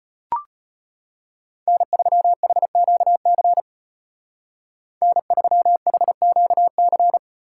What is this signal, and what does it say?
Morse code, heard as a keyed tone, about 31 wpm, 700 Hz. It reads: N3HQC N3HQC
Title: Speed Racing - USA 1x3 Call Signs 25wpm